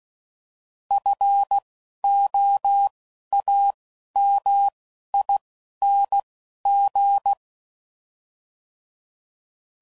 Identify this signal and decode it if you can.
Morse code, heard as a keyed tone, about 16 wpm, 800 Hz. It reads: F O A M I N G
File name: Suffixes / Special INGs / Single INGs